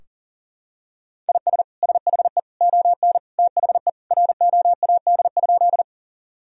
Morse code, heard as a keyed tone, about 40 wpm, 700 Hz. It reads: IS SHE ON THE ROAD?
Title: Sentences from Top 1000 Words - Review Part 2 - 40wpm